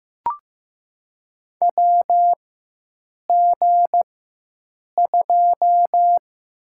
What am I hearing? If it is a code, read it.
Morse code, heard as a keyed tone, about 15 wpm, 700 Hz. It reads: W G 2
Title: USA Call Signs - 4 Characters - 4x Character Spacing - 15wpm